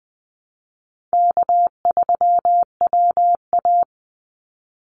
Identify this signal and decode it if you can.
Morse code, heard as a keyed tone, about 20 wpm, 700 Hz. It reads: K3WA
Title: Easy Call Signs Repeated 3x - 20wpm